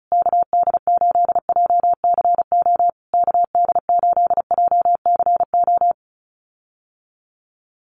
Morse code, heard as a keyed tone, about 34 wpm, 700 Hz. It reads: KD8JCO KD8JCO
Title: USA Call Signs Repeated 3x 35wpm